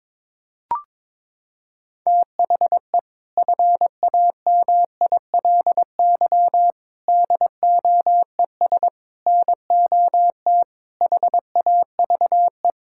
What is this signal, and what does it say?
Morse code, heard as a keyed tone, about 22 wpm, 700 Hz. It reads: THE FAMILY DOES NOT HAVE
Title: Sentences from Top 1600 Words 22wpm